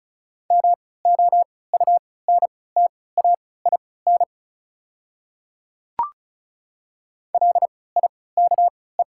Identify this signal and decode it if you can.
Morse code, heard as a keyed tone, about 34 wpm, spaced as for 20 wpm, 700 Hz. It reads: MOUNTAIN LIKE
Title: 3x Character Spacing - Top 300 Words - 35wpm